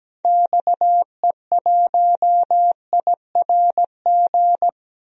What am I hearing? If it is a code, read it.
Morse code, heard as a keyed tone, about 17 wpm, 700 Hz. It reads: XE1IRG
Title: DXCC-100 Call Signs - 6 Characters - Repeated 2x - 17wpm